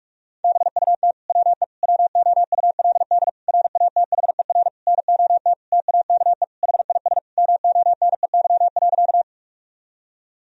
Morse code, heard as a keyed tone, about 46 wpm, 700 Hz. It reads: BUT WE WOULD RATHER NOT TAKE HIS MONEY.
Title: Sentences from Top 1000 Words - Review Part 4 - 45wpm